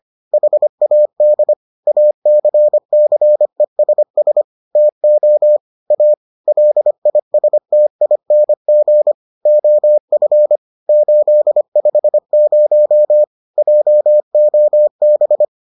Morse code, heard as a keyed tone, about 25 wpm, 600 Hz. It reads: HAD ACCESS TO A LISTING OF 850 JOB